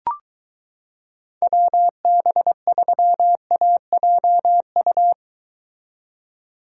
Morse code, heard as a keyed tone, about 23 wpm, 700 Hz. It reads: WB3AJU